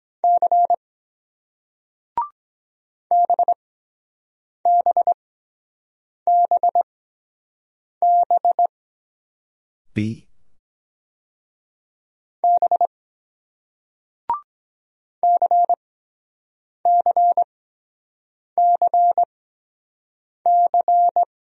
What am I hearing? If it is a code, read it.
Morse code, heard as a keyed tone, about 23 wpm, spaced as for 7 wpm, 700 Hz. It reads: C BBBB B CCCC